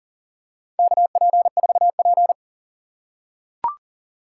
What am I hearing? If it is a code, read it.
Morse code, heard as a keyed tone, about 40 wpm, 700 Hz. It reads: KP4P
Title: USA Call Signs - 4x Character Spacing - 40wpm